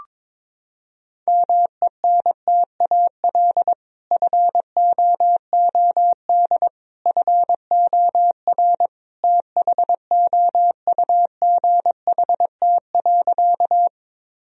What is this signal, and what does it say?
Morse code, heard as a keyed tone, about 22 wpm, 700 Hz. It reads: MENTAL FOOD FOR THOUGHT.
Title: Sentences from Top 1000 Words - Review Part 4 - 22wpm